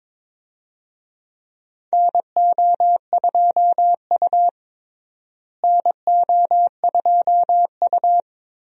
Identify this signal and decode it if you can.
Morse code, heard as a keyed tone, about 22 wpm, 700 Hz. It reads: NO2U NO2U
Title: USA Call Signs - 4 Characters - Repeated 3x - 22wpm